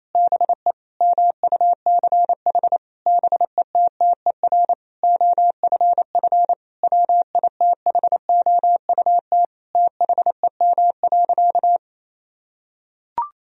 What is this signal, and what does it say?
Morse code, heard as a keyed tone, about 28 wpm, 700 Hz. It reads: BE MUCH BETTER OFF WITHOUT THEM.
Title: Sentences from Top 1000 Words - Review Part 3 - 28wpm